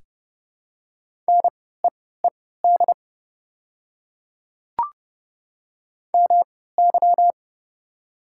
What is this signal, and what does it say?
Morse code, heard as a keyed tone, about 30 wpm, spaced as for 17 wpm, 690 Hz. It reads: NEED MY
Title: 3x Character Spacing - Top 300 Words - 30wpm